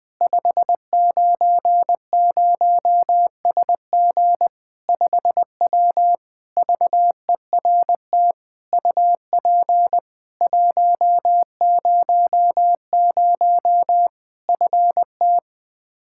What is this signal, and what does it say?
Morse code, heard as a keyed tone, about 20 wpm, 700 Hz. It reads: 590SG 5W VERT UP 100 FT
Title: Practice QSO - Exchange 2 - Info - 20wpm